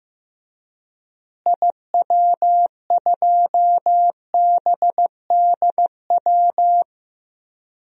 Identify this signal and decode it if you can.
Morse code, heard as a keyed tone, about 16 wpm, 700 Hz. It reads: IW2BDW
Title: DXCC-100 Call Signs - 6 Characters - Repeated 3x - 15wpm